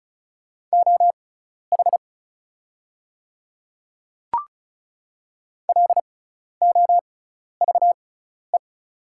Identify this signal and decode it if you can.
Morse code, heard as a keyed tone, about 36 wpm, spaced as for 12 wpm, 700 Hz. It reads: OH LOVE
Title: 6x Character Spacing - Top 500 Words - 35wpm